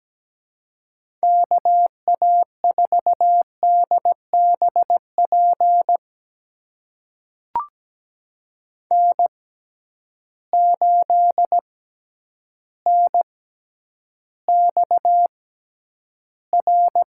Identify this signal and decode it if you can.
Morse code, heard as a keyed tone, about 17 wpm, spaced as for 9 wpm, 700 Hz. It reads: KA4DBP N8NXR
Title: USA Call Signs - 6x Character Spacing - 17wpm